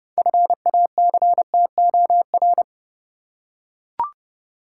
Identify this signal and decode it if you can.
Morse code, heard as a keyed tone, about 30 wpm, 700 Hz. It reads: FACTOR
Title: Six Letter Words - 30wpm